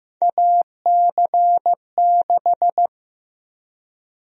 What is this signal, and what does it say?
Morse code, heard as a keyed tone, about 15 wpm, 700 Hz. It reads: AC6
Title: USA Call Sign Prefixes - 15wpm